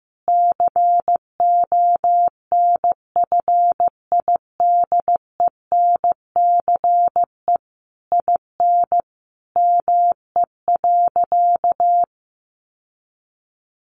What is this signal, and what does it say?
Morse code, heard as a keyed tone, about 15 wpm, 700 Hz. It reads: CONFIDENCE IN ME.
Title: Sentences from Top 1500 Words 15wpm